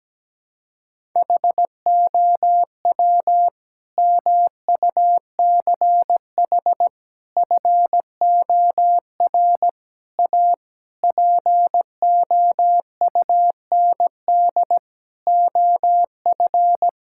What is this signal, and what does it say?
Morse code, heard as a keyed tone, about 17 wpm, 700 Hz. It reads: HOW MUCH FOR A POUND OF
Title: Sentences from Top 1000 Words - Review Part 5 - 17wpm